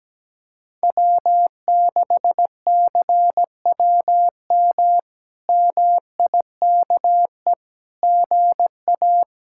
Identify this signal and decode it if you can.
Morse code, heard as a keyed tone, about 17 wpm, 700 Hz. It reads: W6CWM MIKE GA
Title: K1USN SST QSO Practice - 17wpm